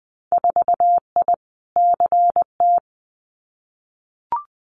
Morse code, heard as a keyed tone, about 20 wpm, 700 Hz. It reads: 4I CT
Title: Winter Field Day Exchanges Encore 20wpm